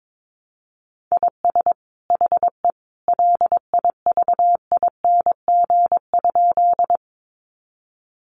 Morse code, heard as a keyed tone, about 22 wpm, 700 Hz. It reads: IS HE LIVING?
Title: Sentences from Top 1100 Words 22wpm